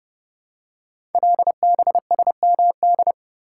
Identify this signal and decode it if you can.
Morse code, heard as a keyed tone, about 28 wpm, 700 Hz. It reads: LBSMD
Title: Code Groups - 5 Letters - 30wpm